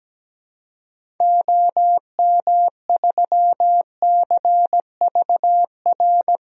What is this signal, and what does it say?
Morse code, heard as a keyed tone, about 17 wpm, 700 Hz. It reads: OM3CVR
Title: DXCC-100 Call Signs - 6 Characters - 17wpm